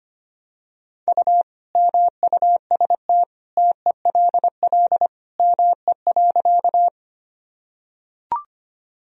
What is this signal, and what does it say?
Morse code, heard as a keyed tone, about 25 wpm, 700 Hz. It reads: U MUST TELL ME.
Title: Sentences from Top 200 Words and Abbreviations 25wpm